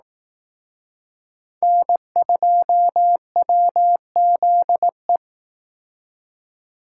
Morse code, heard as a keyed tone, about 19 wpm, 700 Hz. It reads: N2WZE
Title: Speed Racing - USA 1x3 Call Signs 15wpm